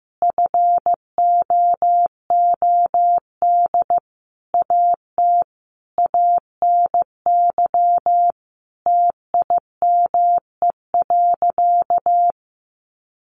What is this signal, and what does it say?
Morse code, heard as a keyed tone, about 15 wpm, 700 Hz. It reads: FOOD AT ANY TIME.
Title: Sentences from Top 1100 Words 15wpm